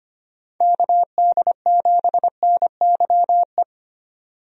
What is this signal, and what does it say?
Morse code, heard as a keyed tone, about 25 wpm, 700 Hz. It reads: KD7NYE
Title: USA Call Signs - 6 Characters - 25wpm